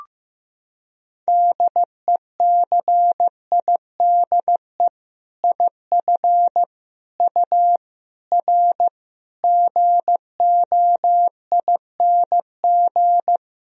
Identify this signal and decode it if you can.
Morse code, heard as a keyed tone, about 15 wpm, 700 Hz. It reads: DECIDE IF U R GOING
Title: Sentences from Top 500 Words and Abbreviations 15wpm